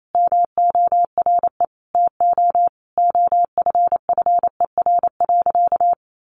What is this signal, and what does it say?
Morse code, heard as a keyed tone, about 28 wpm, 700 Hz. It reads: MORE TO OFFER.
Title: Sentences from Top 1100 Words 28wpm